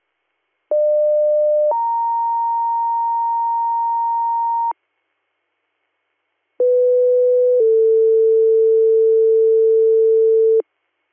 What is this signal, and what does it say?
Two-tone page, heard as a one-second tone, then a three-second tone, about 600 then 930 Hz, about 500 then 450 Hz. It is heard from 0.7 to 10.6 s.